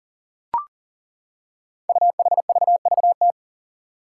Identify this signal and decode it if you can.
Morse code, heard as a keyed tone, about 40 wpm, 700 Hz. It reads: UHVVT
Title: Five Challenging Letter-Number Combinations 40wpm